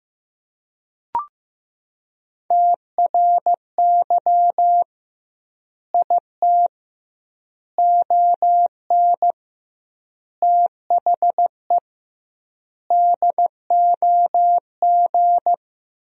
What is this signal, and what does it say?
Morse code, heard as a keyed tone, about 15 wpm, 700 Hz. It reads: TRY IT ON THE DOG
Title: 2x Word-Spacing - Sentences from Top 400 Words - 15wpm